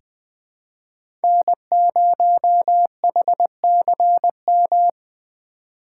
Morse code, heard as a keyed tone, about 21 wpm, 700 Hz. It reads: N0HCM
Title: USA Call Signs - 5 Characters - 3x Character Spacing - 20wpm